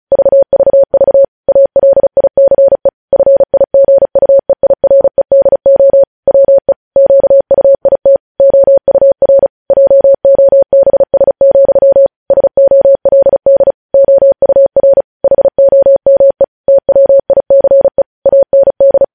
Morse code, heard as a keyed tone, about 36 wpm, 550 Hz. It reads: VVV ALICE FIGUEIREDO WE QUIT OUR JOBS, SOLD OUR HOME TWICE AND
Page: news headlines in morse code at 35 wpm